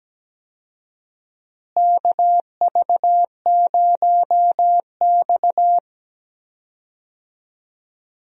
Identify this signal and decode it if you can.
Morse code, heard as a keyed tone, about 17 wpm, 700 Hz. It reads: KV0X
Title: USA Call Signs - 4 Characters - Repeated 2x - 17wpm